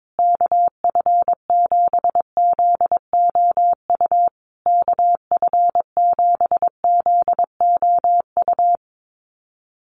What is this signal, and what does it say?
Morse code, heard as a keyed tone, about 22 wpm, 700 Hz. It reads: KF7ZOU KF7ZOU
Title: USA Call Signs Repeated 3x 22wpm